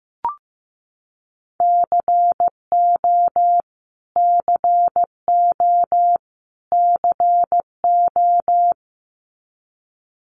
Morse code, heard as a keyed tone, about 15 wpm, 700 Hz. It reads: CO CO CO
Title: US States Repeated 3x 15wpm